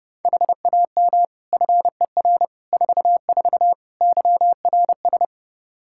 Morse code, heard as a keyed tone, about 29 wpm, 700 Hz. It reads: HAM FER 44 YRS
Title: Practice QSO - Exchange 3 - More Info - 30wpm